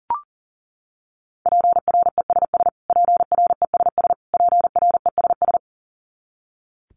Morse code, heard as a keyed tone, about 40 wpm, 700 Hz. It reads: PRESS PRESS PRESS
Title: Top 500 Words Repeated 40wpm